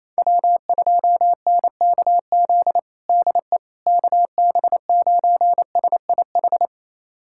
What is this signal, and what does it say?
Morse code, heard as a keyed tone, about 28 wpm, 700 Hz. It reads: W2NKZ DE KB9SIH
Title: USA Callsign DE Callsign 28wpm